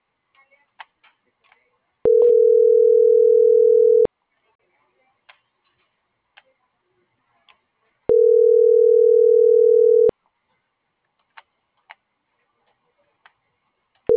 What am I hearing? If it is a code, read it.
Ringback tone (North America), 440 + 480 Hz.
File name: CENTRAL GENERAL / out karla.sky